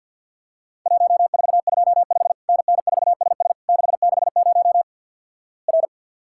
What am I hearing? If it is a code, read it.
Morse code, heard as a keyed tone, about 50 wpm, 700 Hz. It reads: 1425 NN4SS 660 R
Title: Worked All Europe DX Contest - QTC Exchange - 50wpm